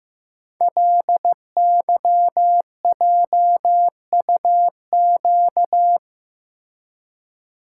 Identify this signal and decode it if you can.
Morse code, heard as a keyed tone, about 15 wpm, 700 Hz. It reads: LYJUQ
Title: Code Groups - 5 Letters - 15wpm